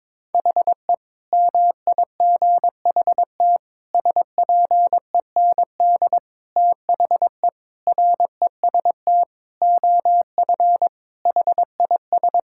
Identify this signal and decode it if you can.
Morse code, heard as a keyed tone, about 22 wpm, 700 Hz. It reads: HE MIGHT SPEND THE REST OF HIS
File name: practice / 22wpm / Sentences with 8-Word Phrases - 22wpm